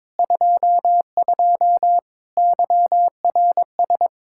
Morse code, heard as a keyed tone, about 22 wpm, 700 Hz. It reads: 22 YRS
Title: Practice QSO - Exchange 3 - More Info - 22wpm